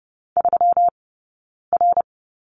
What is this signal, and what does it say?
Morse code, heard as a keyed tone, about 31 wpm, 700 Hz. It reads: 3 R
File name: Lesson 20d - Number 3 - Two Characters 30wpm